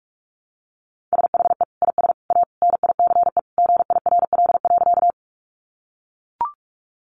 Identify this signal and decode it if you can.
Morse code, heard as a keyed tone, about 44 wpm, 700 Hz. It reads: SHE IS A NICE GIRL.
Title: Sentences from Top 1100 Words 45wpm